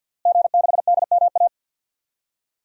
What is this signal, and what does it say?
Morse code, heard as a keyed tone, about 46 wpm, 700 Hz. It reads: G6DMA
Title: DXCC-100 Call Signs - 5 Characters - Repeated 3x - 50wpm